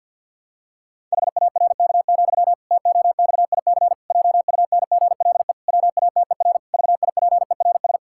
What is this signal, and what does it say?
Morse code, heard as a keyed tone, about 50 wpm, 700 Hz. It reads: HARK, TOXIC JUNGLE WATER VIPERS
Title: Pangram Challenge v2 - 50wpm